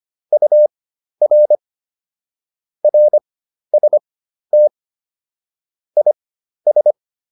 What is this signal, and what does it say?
Morse code, heard as a keyed tone, about 25 wpm, spaced as for 12 wpm, 600 Hz. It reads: UR RST IS